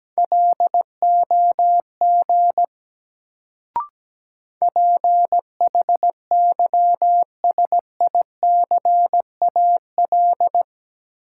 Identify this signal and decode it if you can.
Morse code, heard as a keyed tone, about 17 wpm, 700 Hz. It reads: LOG PHYSICAL
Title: Sentences from Top 1400 Words - New Word Drill Alt - 17wpm